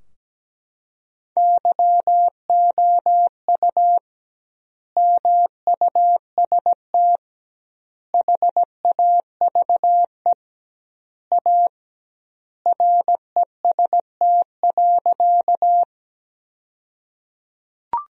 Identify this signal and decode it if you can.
Morse code, heard as a keyed tone, about 17 wpm, 700 Hz. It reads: YOU MUST HAVE A REST.
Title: 2x Word-Spacing - Sentences from Top 500 Words - 17wpm